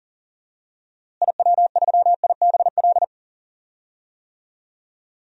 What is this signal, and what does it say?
Morse code, heard as a keyed tone, about 39 wpm, 700 Hz. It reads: IW3IBL
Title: DXCC-100 Call Signs - 6 Characters - Repeated 3x - 40wpm